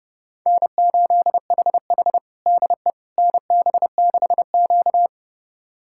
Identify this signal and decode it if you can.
Morse code, heard as a keyed tone, about 29 wpm, 700 Hz. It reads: N8HH DE NB6Q
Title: USA Callsign DE Callsign - 4 Characters - 30wpm